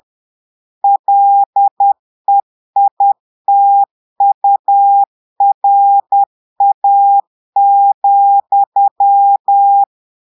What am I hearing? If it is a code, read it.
Morse code, heard as a keyed tone, about 10 wpm, 800 Hz. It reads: LEITURA,
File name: Portugais - texte en clair 1 - vitesse - 10 mots minute